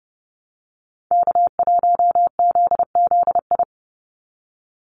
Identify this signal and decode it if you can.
Morse code, heard as a keyed tone, about 30 wpm, 700 Hz. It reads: K1ZZI
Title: Call Signs - 30wpm